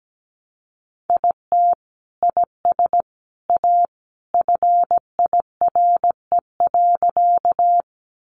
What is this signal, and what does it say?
Morse code, heard as a keyed tone, about 17 wpm, 700 Hz. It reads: IT IS A FIRE.
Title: Sentences from Top 400 Words 17wpm